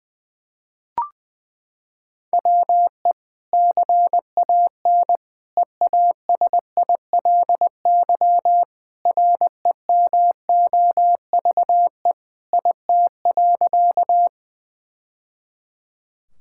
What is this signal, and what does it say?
Morse code, heard as a keyed tone, about 21 wpm, 700 Hz. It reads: WE CAN EASILY REMOVE IT.
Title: Sentences from Top 1000 Words - Review Part 5 - 20wpm